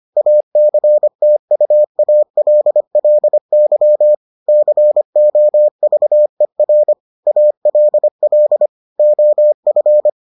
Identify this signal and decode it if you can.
Morse code, heard as a keyed tone, about 25 wpm, 600 Hz. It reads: ACTUALLY COVER ALL OF